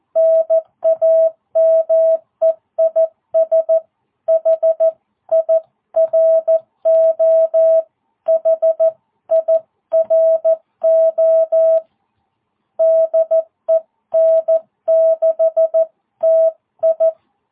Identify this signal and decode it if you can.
Morse code, heard as a keyed tone, about 14 wpm, 660 Hz. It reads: NAME IS HIRO HIRO DE N6TI